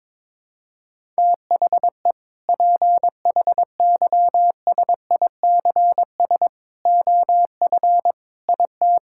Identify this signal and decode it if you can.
Morse code, heard as a keyed tone, about 21 wpm, 700 Hz. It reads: THE PHYSICS OF IT
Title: Sentences from Top 1000 Words - Review Part 5 - 22wpm